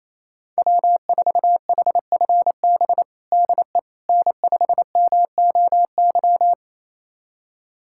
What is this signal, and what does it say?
Morse code, heard as a keyed tone, about 27 wpm, 700 Hz. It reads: W4HFB DE N5MOY
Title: USA Callsign DE Callsign - 5 Characters - 28wpm